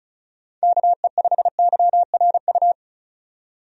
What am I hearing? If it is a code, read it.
Morse code, heard as a keyed tone, about 36 wpm, 700 Hz. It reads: KE5YRU
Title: USA Call Signs - 6 Characters - 35wpm